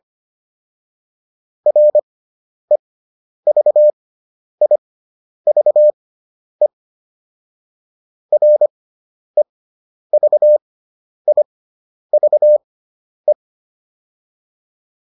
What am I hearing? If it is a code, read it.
Morse code, heard as a keyed tone, about 25 wpm, spaced as for 10 wpm, 600 Hz. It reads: REVIVE REVIVE